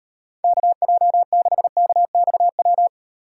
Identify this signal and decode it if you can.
Morse code, heard as a keyed tone, about 38 wpm, 700 Hz. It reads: KJ6KXW